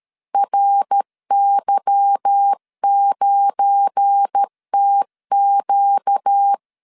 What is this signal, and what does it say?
Morse code, heard as a keyed tone, about 13 wpm, 800 Hz. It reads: RY9TQ
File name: LEZIONI / 8 YJ45 / 8 Y 12wpm 2228